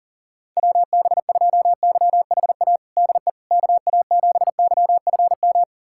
Morse code, heard as a keyed tone, about 40 wpm, 700 Hz. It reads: WB2YHA DE KA7YFM
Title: USA Callsign DE Callsign - 6 Characters - 40wpm